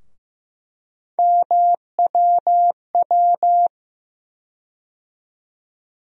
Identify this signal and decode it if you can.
Morse code, heard as a keyed tone, about 15 wpm, 700 Hz. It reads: MWW
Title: USA Call Sign Postfixes - 3 Letters 15wpm